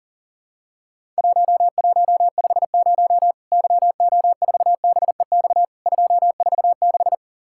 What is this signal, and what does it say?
Morse code, heard as a keyed tone, about 40 wpm, 700 Hz. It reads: 1150 YO4BEX 246
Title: Worked All Europe DX Contest - QTC Exchange - 40wpm